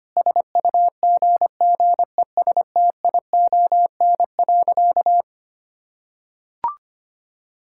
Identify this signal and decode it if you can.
Morse code, heard as a keyed tone, about 25 wpm, 700 Hz. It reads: SUGGESTION.
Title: Sentences from Top 1600 Words 25wpm